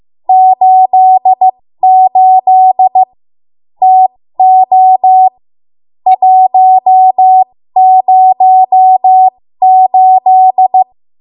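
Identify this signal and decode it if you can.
Morse code, heard as a keyed tone, about 15 wpm, spaced as for 13 wpm, 750 Hz. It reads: 88 TO 108
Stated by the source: 13 wpm Morse code practice archive transmission.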